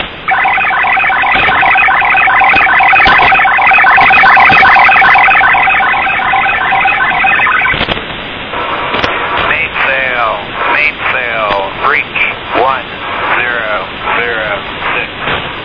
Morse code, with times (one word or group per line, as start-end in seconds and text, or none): none